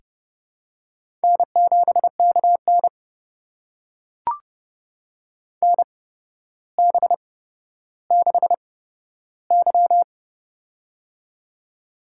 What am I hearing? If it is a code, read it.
Morse code, heard as a keyed tone, about 30 wpm, spaced as for 13 wpm, 700 Hz. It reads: N7KN NB6Y
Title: USA Call Signs - 4 Characters - 8x Character Spacing - 30wpm